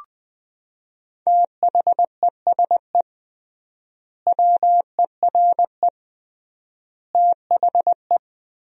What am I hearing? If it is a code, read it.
Morse code, heard as a keyed tone, about 20 wpm, 700 Hz. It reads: THESE WERE THE
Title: 3x Word-Spacing - Sentences from Top 300 Words - 20wpm